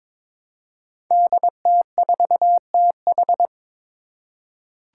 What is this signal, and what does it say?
Morse code, heard as a keyed tone, about 22 wpm, 700 Hz. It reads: DT4TH
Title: Five Challenging Letter-Number Combinations 22wpm